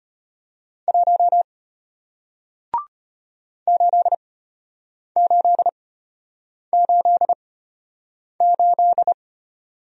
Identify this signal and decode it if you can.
Morse code, heard as a keyed tone, about 33 wpm, spaced as for 8 wpm, 700 Hz. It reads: 1 8888